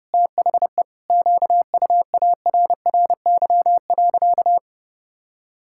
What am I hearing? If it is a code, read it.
Morse code, heard as a keyed tone, about 32 wpm, 700 Hz. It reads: THE QUARRY.
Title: Pangram Challenge v2 - 30wpm